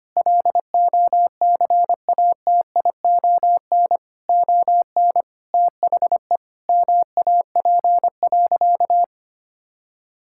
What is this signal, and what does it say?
Morse code, heard as a keyed tone, about 25 wpm, 700 Hz. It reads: LOCATION ON THE MAP.
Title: Sentences from Top 1000 Words - Review Part 6 - 25wpm